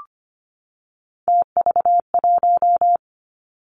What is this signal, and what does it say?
Morse code, heard as a keyed tone, about 25 wpm, 700 Hz. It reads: TV1
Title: Three Challenging Letter-Number Combinations 25wpm